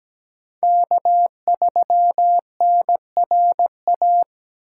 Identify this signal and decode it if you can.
Morse code, heard as a keyed tone, about 20 wpm, 700 Hz. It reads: K3NRA
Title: Speed Racing - USA 1x3 Call Signs 17wpm